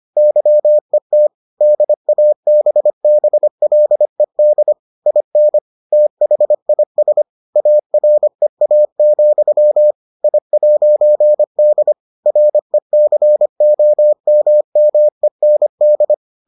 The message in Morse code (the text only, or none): YET DABBLED IN THIS AREA, I'D RECOMMEND